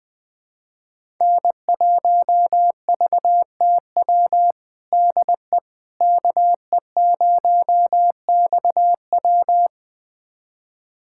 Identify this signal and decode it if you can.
Morse code, heard as a keyed tone, about 20 wpm, 700 Hz. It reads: N1VTW DE KE0XW
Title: USA Callsign DE Callsign - 5 Characters - 20wpm